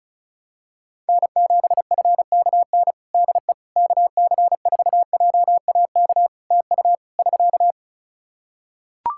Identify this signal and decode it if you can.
Morse code, heard as a keyed tone, about 34 wpm, 700 Hz. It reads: N7FKN DE KC4JAK TU <SK>
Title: Practice QSO - Exchange 4 - Ending - 35wpm